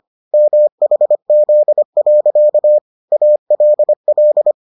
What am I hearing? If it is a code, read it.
Morse code, heard as a keyed tone, about 25 wpm, 600 Hz. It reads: MHZ. ALL